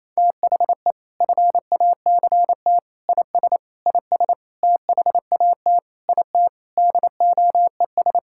Morse code, heard as a keyed tone, about 28 wpm, 700 Hz. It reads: THE FACT IS IS THAT IT DOES
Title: Sentences from Top 1600 Words 28wpm